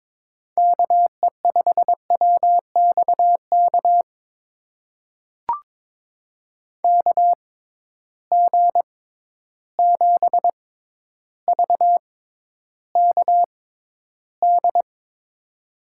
Morse code, heard as a keyed tone, about 23 wpm, spaced as for 11 wpm, 700 Hz. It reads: KE5WXK KG7VKD
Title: USA Call Signs - 6 Characters - 6x Character Spacing - 22wpm